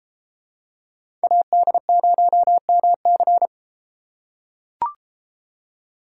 Morse code, heard as a keyed tone, about 30 wpm, 700 Hz. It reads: AD0MC